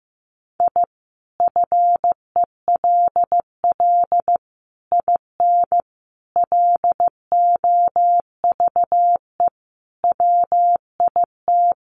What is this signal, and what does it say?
Morse code, heard as a keyed tone, about 15 wpm, 700 Hz. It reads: I FELL IN LOVE WIT
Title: Sets of 5 Words Encore 15wpm